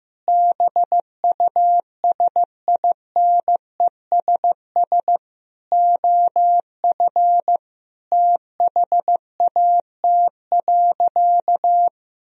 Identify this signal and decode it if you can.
Morse code, heard as a keyed tone, about 15 wpm, 700 Hz. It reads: BUSINESS OF THAT.
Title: Sentences from Top 1000 Words - Review Part 4 - 15wpm